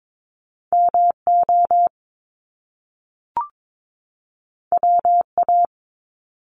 Morse code, heard as a keyed tone, about 22 wpm, 700 Hz. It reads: MO WA
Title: Top Two Letter Combinations in Words 22wpm